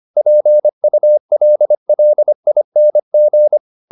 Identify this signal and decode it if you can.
Morse code, heard as a keyed tone, about 25 wpm, 600 Hz. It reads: PULLING
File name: the old man and the sea part 1 25wpm